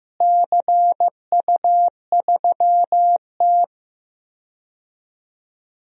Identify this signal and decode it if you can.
Morse code, heard as a keyed tone, about 15 wpm, 700 Hz. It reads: CU3T
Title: DXCC-100 Call Signs - 4 Characters - Repeated 3x - 15wpm